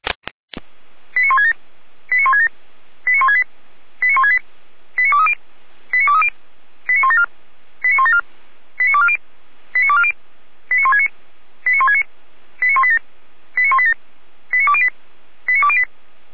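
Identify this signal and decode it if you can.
Five-tone selective call (ZVEI), heard as quick runs of five tones, tones of about 70 ms. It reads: 89167 89167 89168 89168 89230 89230 89174 89174 89240 89240 89169 89169 89187 89187 89208 89208
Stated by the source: ZVEI five-tone selective call received by SDR.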